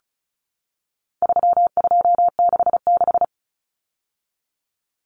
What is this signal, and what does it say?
Morse code, heard as a keyed tone, about 34 wpm, 700 Hz. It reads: 3266
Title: Numbers - 4-Digit 35wpm